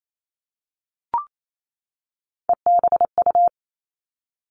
Morse code, heard as a keyed tone, about 28 wpm, 700 Hz. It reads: EBU
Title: Three Challenging Letter-Number Combinations 28wpm